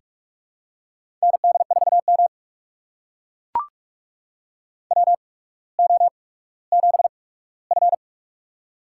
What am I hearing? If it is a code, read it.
Morse code, heard as a keyed tone, about 44 wpm, spaced as for 19 wpm, 700 Hz. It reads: ND4M WO7F
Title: USA Call Signs - 4 Characters - 8x Character Spacing - 45wpm